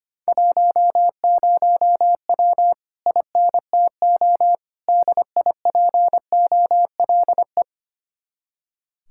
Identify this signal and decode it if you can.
Morse code, heard as a keyed tone, about 30 wpm, 700 Hz. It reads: 10W INTO DIPOLE
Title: Speed-Racing Common QSO Phrases Encore - 25wpm